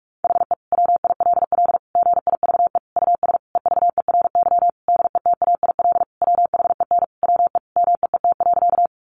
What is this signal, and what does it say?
Morse code, heard as a keyed tone, about 44 wpm, 700 Hz. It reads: HE WILL GIVE US EVERY DETAIL WHEN WE MEET.
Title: Sentences from Top 1500 Words 45wpm